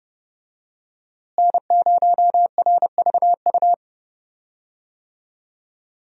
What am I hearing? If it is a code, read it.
Morse code, heard as a keyed tone, about 30 wpm, 700 Hz. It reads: N0RVU
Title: USA Call Signs - 5 Characters - Repeated 3x - 30wpm